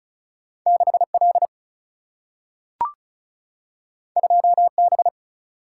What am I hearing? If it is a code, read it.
Morse code, heard as a keyed tone, about 36 wpm, 700 Hz. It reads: 6 L 2 B
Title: Single-Letter Single-Number 35wpm